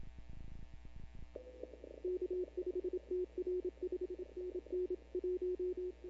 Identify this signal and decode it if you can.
Morse code, heard as a keyed tone, about 27 wpm, 360 Hz. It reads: K5TR 5NN 1
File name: CW / 2005-k5tr arrl cw dx -15:00